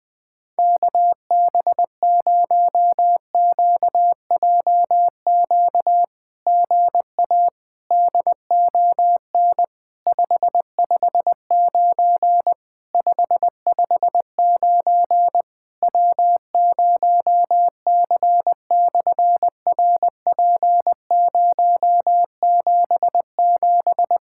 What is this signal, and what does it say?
Morse code, heard as a keyed tone, about 20 wpm, 700 Hz. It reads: KB0QJQ GA DON 559 559 W0C/RP077
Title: SOTA QSO Practice 20wpm